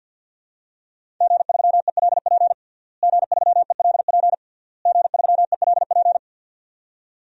Morse code, heard as a keyed tone, about 51 wpm, 700 Hz. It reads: G3ELP G3ELP G3ELP
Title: DXCC-100 Call Signs - 5 Characters - Repeated 3x - 50wpm